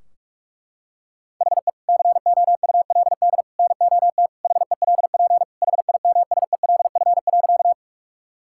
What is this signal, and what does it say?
Morse code, heard as a keyed tone, about 41 wpm, 700 Hz. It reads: HE COULD NOT HELP HIMSELF.
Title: Sentences from Top 1600 Words 45wpm